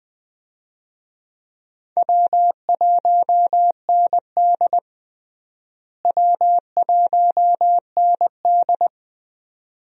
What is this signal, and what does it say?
Morse code, heard as a keyed tone, about 20 wpm, 700 Hz. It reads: W1ND W1ND
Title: USA Call Signs - 4 Characters - Repeated 2x - 20wpm